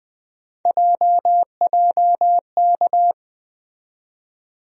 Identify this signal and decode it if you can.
Morse code, heard as a keyed tone, about 20 wpm, 700 Hz. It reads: JJK
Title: USA Call Sign Postfixes - 3 Letters 20wpm